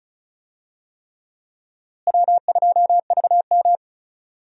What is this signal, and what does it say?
Morse code, heard as a keyed tone, about 36 wpm, 700 Hz. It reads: W2VM
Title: USA Call Signs - 4 Characters - Repeated 3x - 35wpm